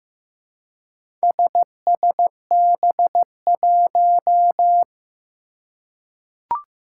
Morse code, heard as a keyed tone, about 15 wpm, 700 Hz. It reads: SSB1
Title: Four Challenging Letter-Number Combinations 15wpm